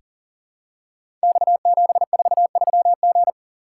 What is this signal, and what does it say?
Morse code, heard as a keyed tone, about 42 wpm, 700 Hz. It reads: X743G
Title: Code Groups - 5 Letter:Numbers - 40wpm